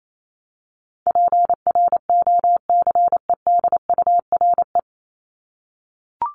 Morse code, heard as a keyed tone, about 28 wpm, 700 Hz. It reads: PROCEDURE
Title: Top 1751-2000 Words 28wpm